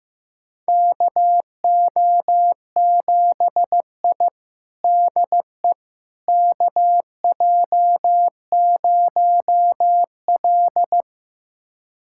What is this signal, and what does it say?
Morse code, heard as a keyed tone, about 15 wpm, 700 Hz. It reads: KO7I DE KJ0L
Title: USA Callsign DE Callsign - 4 Characters - 15wpm